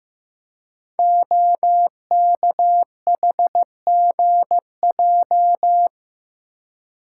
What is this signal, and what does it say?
Morse code, heard as a keyed tone, about 15 wpm, 700 Hz. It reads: OKHGJ
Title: Code Groups - 5 Letter:Numbers - 15wpm